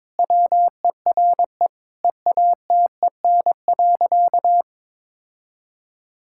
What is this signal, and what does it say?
Morse code, heard as a keyed tone, about 21 wpm, 700 Hz. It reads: WERE EATEN.
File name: practice / 22wpm / Sentences with 2-Word Phrases - 22wpm